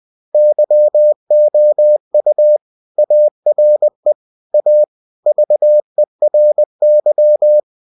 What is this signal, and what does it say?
Morse code, heard as a keyed tone, about 20 wpm, 600 Hz. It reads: YOU ARE A VERY
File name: cw-books / winnie-the-pooh / winnie-20wpm-page-0062